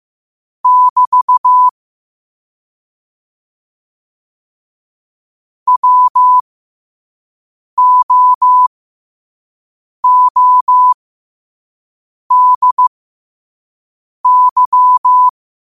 Morse code, heard as a keyed tone, about 15 wpm, spaced as for 5 wpm, 1000 Hz. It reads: <BT> WOODY
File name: qotdfiles / 5wpm